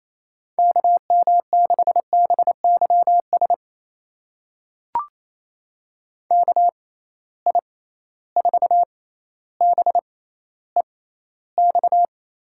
Morse code, heard as a keyed tone, about 28 wpm, spaced as for 14 wpm, 700 Hz. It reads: KM6BYS KI4BEX
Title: USA Call Signs - 6 Characters - 6x Character Spacing - 28wpm